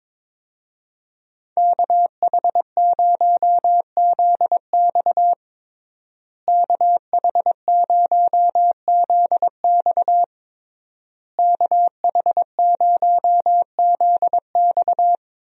Morse code, heard as a keyed tone, about 21 wpm, 700 Hz. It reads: KH0ZX KH0ZX KH0ZX
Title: USA Call Signs - 5 Characters - Repeated 3x - 22wpm